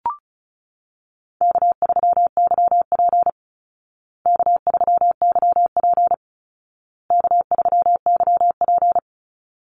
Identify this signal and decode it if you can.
Morse code, heard as a keyed tone, about 34 wpm, 700 Hz. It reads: K3YP K3YP K3YP
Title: Call Signs Repeated 2x - 35wpm